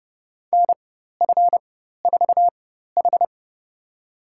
Morse code, heard as a keyed tone, about 30 wpm, spaced as for 14 wpm, 700 Hz. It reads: NF4H
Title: USA Call Signs - 4x Character Spacing - 30wpm